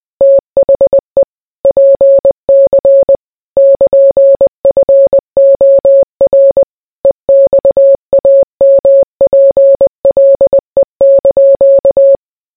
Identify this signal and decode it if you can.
Morse code, heard as a keyed tone, about 20 wpm, 550 Hz. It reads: THE PC <KN>FOR EXAMPLE)